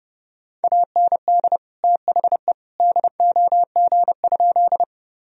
Morse code, heard as a keyed tone, about 30 wpm, 700 Hz. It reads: AND THE DOG?
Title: Sentences from Top 1000 Words - Review Part 2 - 30wpm